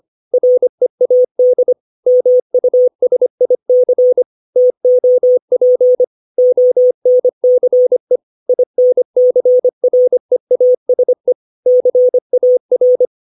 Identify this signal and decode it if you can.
Morse code, heard as a keyed tone, about 25 wpm, 500 Hz. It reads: READ MUSIC TOP ONCE INCREASE CAR
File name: 632 words / 632 Words 25WPM